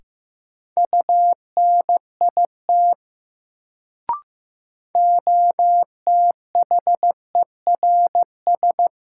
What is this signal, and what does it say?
Morse code, heard as a keyed tone, about 15 wpm, 700 Hz. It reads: UNIT OTHERS
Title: Sentences from Top 1000 Words - Review Part 2 - New Word Drill - 15wpm